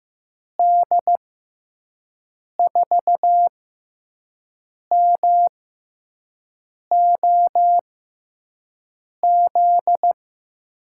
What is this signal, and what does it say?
Morse code, heard as a keyed tone, about 15 wpm, spaced as for 5 wpm, 700 Hz. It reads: D4MOZ
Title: USA Call Signs - 6 Characters - 6x Character Spacing - 15wpm